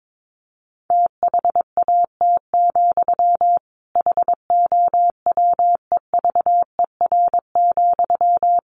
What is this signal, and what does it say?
Morse code, heard as a keyed tone, about 22 wpm, 700 Hz. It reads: THAT, HOWEVER,
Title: Sentences from Top 600 Words 22wpm